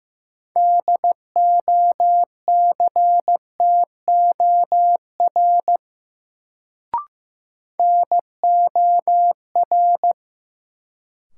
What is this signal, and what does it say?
Morse code, heard as a keyed tone, about 15 wpm, 700 Hz. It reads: DOCTOR NOR
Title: Sentences from Top 1500 Words - New Word Drill - 15wpm